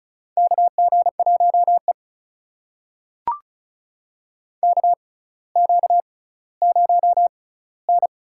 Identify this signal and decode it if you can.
Morse code, heard as a keyed tone, about 36 wpm, spaced as for 18 wpm, 700 Hz. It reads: KG1E KQ0N
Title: USA Call Signs - 4 Characters - 6x Character Spacing - 35wpm